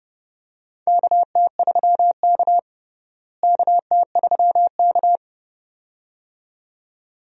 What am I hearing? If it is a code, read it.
Morse code, heard as a keyed tone, about 30 wpm, 700 Hz. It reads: KT3K KT3K
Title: USA Call Signs - 4 Characters - Repeated 2x - 30wpm